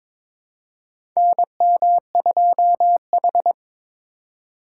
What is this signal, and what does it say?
Morse code, heard as a keyed tone, about 22 wpm, 700 Hz. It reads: NM2H
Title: USA Call Signs - 4 Characters - 3x Character Spacing - 22wpm